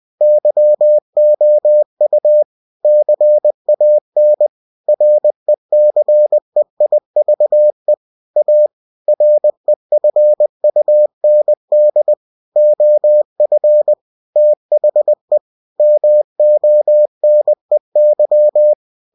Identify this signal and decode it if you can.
Morse code, heard as a keyed tone, about 20 wpm, 600 Hz. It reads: YOU CAN RECEIVE A REFUND OF THE MONEY
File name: cw-books / winnie-the-pooh / winnie-20wpm-page-0119